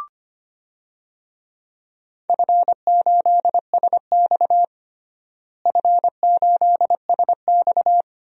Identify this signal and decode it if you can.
Morse code, heard as a keyed tone, about 25 wpm, 700 Hz. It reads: F8SX F8SX
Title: DXCC-100 Call Signs - 4 Characters - Repeated 2x - 25wpm